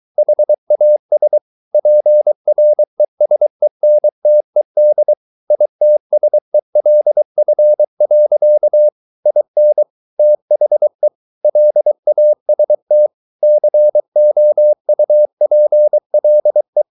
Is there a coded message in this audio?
Yes: Morse code from 0.2 to 16.8 s, heard as a keyed tone, about 23 wpm, 600 Hz. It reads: HAS PRESENTED ITSELF. IN THE LAST COUPLE